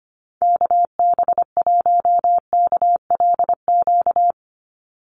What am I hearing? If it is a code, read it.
Morse code, heard as a keyed tone, about 25 wpm, 690 Hz. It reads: KB1KLQ
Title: USA Call Signs 25wpm